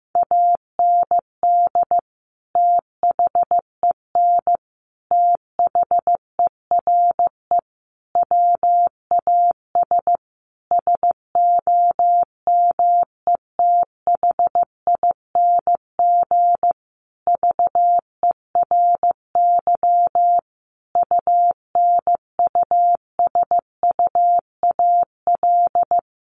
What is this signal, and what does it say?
Morse code, heard as a keyed tone, about 15 wpm, 700 Hz. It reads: AND THEN THERE WAS SOMETHING VERY UNUSUAL
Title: Sentences from Top 1400 Words 15wpm